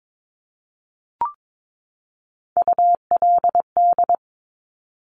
Morse code, heard as a keyed tone, about 22 wpm, 700 Hz. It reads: ULD
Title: Top Three Letter Combinations in Words 22wpm